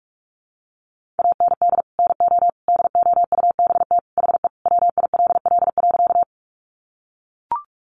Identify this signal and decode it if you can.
Morse code, heard as a keyed tone, about 44 wpm, 700 Hz. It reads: AND NO DOUBT HE WILL.
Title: Sentences from Top 1500 Words 45wpm